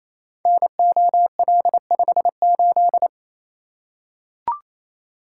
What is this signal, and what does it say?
Morse code, heard as a keyed tone, about 28 wpm, 700 Hz. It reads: NOL58
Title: Code Groups - 5 Letter:Numbers - 28wpm